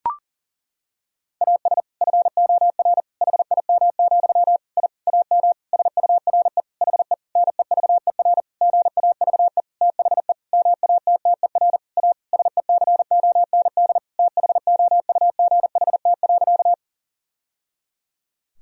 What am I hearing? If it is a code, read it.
Morse code, heard as a keyed tone, about 40 wpm, 700 Hz. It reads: AS FOR HIM, I AM SURE HE NEVER GAVE THE MATTER A SECOND THOUGHT.
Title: Sentences from Top 1000 Words - Review Part 2 - 40wpm